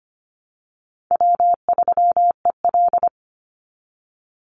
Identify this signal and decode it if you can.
Morse code, heard as a keyed tone, about 25 wpm, 700 Hz. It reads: W3EL
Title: Call Signs Repeated 2x - 25wpm